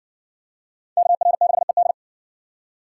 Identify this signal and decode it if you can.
Morse code, heard as a keyed tone, about 52 wpm, 700 Hz. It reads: KR6ED